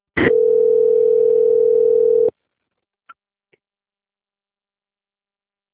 Ringback tone (North America), 440 + 480 Hz.